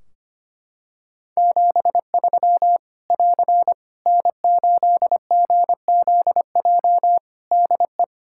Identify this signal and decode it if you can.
Morse code, heard as a keyed tone, about 25 wpm, 700 Hz. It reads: 73 <AR> N8GZJ DE
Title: Practice QSO - Exchange 4 - Ending - 25wpm